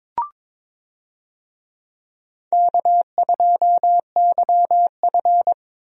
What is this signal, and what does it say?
Morse code, heard as a keyed tone, about 22 wpm, 700 Hz. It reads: K2YF
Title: USA Call Signs - 4 Characters - Repeated 2x - 22wpm